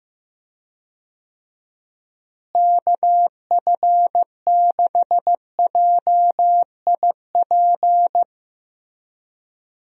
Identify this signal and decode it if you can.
Morse code, heard as a keyed tone, about 15 wpm, 700 Hz. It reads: KF6JIP
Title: USA Call Signs - 6 Characters - Repeated 3x - 15wpm